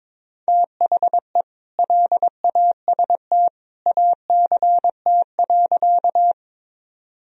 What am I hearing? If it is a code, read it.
Morse code, heard as a keyed tone, about 22 wpm, 700 Hz. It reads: THE LAST ACT.
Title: Sentences from Top 1000 Words - Review Part 1 - 22wpm